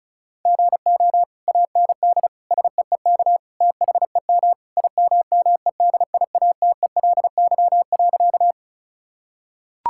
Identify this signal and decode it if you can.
Morse code, heard as a keyed tone, about 36 wpm, 700 Hz. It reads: GO AND SEEK THEM IMMEDIATELY.
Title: Sentences from Top 1600 Words 35wpm